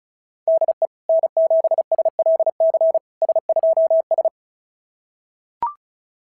Morse code, heard as a keyed tone, about 31 wpm, 660 Hz. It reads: DE N7SLC S2S
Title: SOTA QSO Practice 35wpm